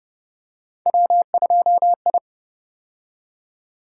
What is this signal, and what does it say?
Morse code, heard as a keyed tone, about 32 wpm, 700 Hz. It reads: W2I